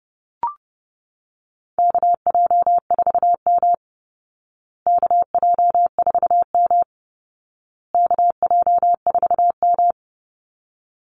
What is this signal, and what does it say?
Morse code, heard as a keyed tone, about 30 wpm, 700 Hz. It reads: KJ4M KJ4M KJ4M
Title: Call Signs Repeated 2x - 30wpm